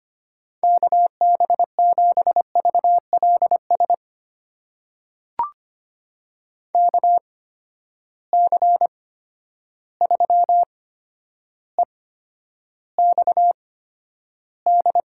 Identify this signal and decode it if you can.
Morse code, heard as a keyed tone, about 25 wpm, spaced as for 11 wpm, 700 Hz. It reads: KB7VLS KC3EXD
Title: USA Call Signs - 6 Characters - 8x Character Spacing - 25wpm